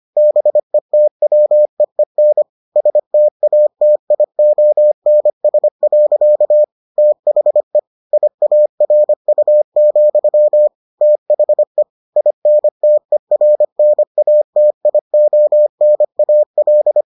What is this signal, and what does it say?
Morse code, heard as a keyed tone, about 25 wpm, 600 Hz. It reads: BETWEEN STATIONS. THE IARU, THE INTERNATIONAL